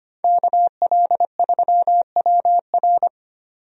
Morse code, heard as a keyed tone, about 25 wpm, 700 Hz. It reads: KL3WR
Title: USA Call Signs - 5 Characters - 25wpm